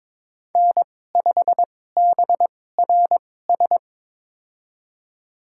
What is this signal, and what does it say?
Morse code, heard as a keyed tone, about 22 wpm, spaced as for 16 wpm, 700 Hz. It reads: N5BRS
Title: USA Call Signs - 5 Characters - 2x Character Spacing - 22wpm